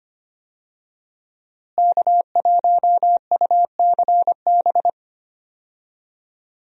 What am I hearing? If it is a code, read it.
Morse code, heard as a keyed tone, about 25 wpm, 700 Hz. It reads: K1UCB
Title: USA Call Signs - 5 Characters - Repeated 2x - 25wpm